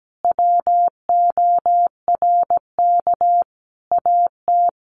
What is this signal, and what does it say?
Morse code, heard as a keyed tone, about 17 wpm, 700 Hz. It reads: WORK AT
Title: Sentences from Top 1300 Words 17wpm